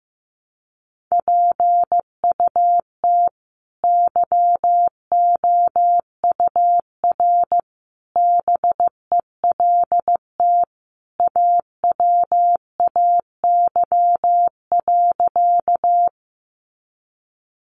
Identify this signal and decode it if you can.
Morse code, heard as a keyed tone, about 15 wpm, 700 Hz. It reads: PUT YOUR BELT AWAY.
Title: Sentences from Top 1400 Words 15wpm